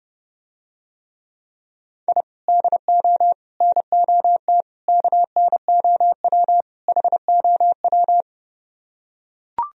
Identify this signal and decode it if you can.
Morse code, heard as a keyed tone, about 32 wpm, 700 Hz. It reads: I DO NOT KNOW HOW